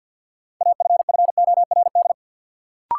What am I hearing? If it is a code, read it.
Morse code, heard as a keyed tone, about 50 wpm, 700 Hz. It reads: AFFORD
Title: Sentences from Top 1100 Words - New Word Drill - 50wpm